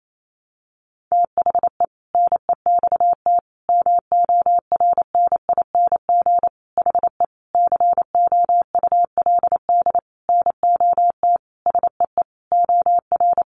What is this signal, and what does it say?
Morse code, heard as a keyed tone, about 28 wpm, 700 Hz. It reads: THE NEXT MORNING HE COULD NOT SEE OR
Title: Sentences from Top 600 Words 28wpm